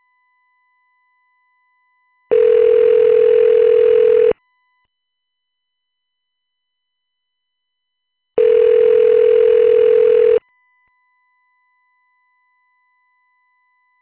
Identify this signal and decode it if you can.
Ringback tone (North America), 440 + 480 Hz.